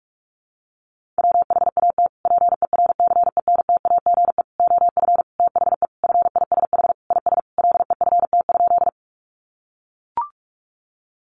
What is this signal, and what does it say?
Morse code, heard as a keyed tone, about 44 wpm, 700 Hz. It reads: WHAT PERCENTAGE OF THE FISH IS LEFT?
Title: Sentences from Top 1300 Words 45wpm